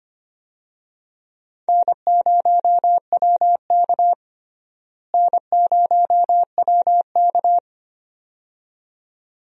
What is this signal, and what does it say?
Morse code, heard as a keyed tone, about 25 wpm, 700 Hz. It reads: N0WK N0WK
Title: USA Call Signs - 4 Characters - Repeated 2x - 25wpm